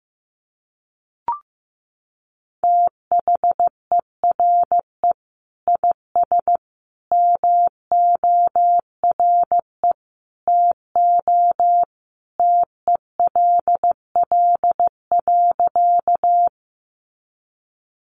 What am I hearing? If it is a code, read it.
Morse code, heard as a keyed tone, about 15 wpm, 700 Hz. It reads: THERE IS MORE TO TELL.
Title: Sentences from Top 200 Words 15wpm